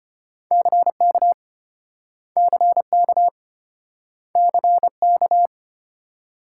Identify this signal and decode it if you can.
Morse code, heard as a keyed tone, about 34 wpm, 700 Hz. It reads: CK CK CK